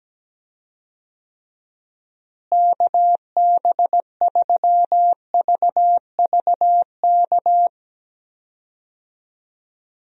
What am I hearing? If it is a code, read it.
Morse code, heard as a keyed tone, about 17 wpm, 700 Hz. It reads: KB3VVK
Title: USA Call Signs - 6 Characters - Repeated 3x - 17wpm